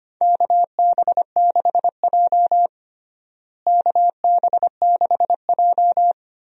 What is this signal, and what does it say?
Morse code, heard as a keyed tone, about 25 wpm, 700 Hz. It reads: KB6J KB6J
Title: USA Call Signs - 4 Characters - Repeated 2x - 25wpm